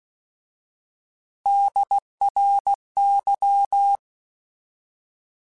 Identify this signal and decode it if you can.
Morse code, heard as a keyed tone, about 16 wpm, 800 Hz. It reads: DRY
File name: Alphabetical / Alphabetical mp3s / Ds @15